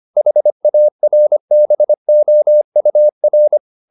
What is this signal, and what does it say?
Morse code, heard as a keyed tone, about 25 wpm, 600 Hz. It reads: HARBOUR
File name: the old man and the sea part 1 25wpm